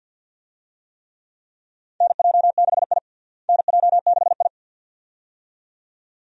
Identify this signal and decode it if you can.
Morse code, heard as a keyed tone, about 46 wpm, 700 Hz. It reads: NJ6I NJ6I
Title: USA Call Signs - 4 Characters - Repeated 2x - 50wpm